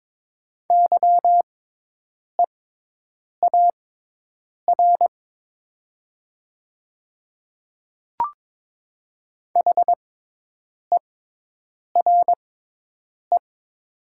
Morse code, heard as a keyed tone, about 22 wpm, spaced as for 8 wpm, 700 Hz. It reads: YEAR HERE
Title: 6x Character Spacing - Top 200 Words - 22wpm